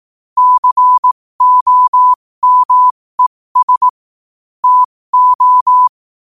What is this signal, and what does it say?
Morse code, heard as a keyed tone, about 18 wpm, spaced as for 15 wpm, 1000 Hz. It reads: COMES TO
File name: qotdfiles / 15wpm